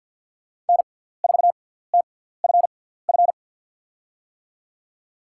Morse code, heard as a keyed tone, about 47 wpm, spaced as for 17 wpm, 700 Hz. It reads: N4TFF
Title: USA Call Signs - 5 Characters - 6x Character Spacing - 50wpm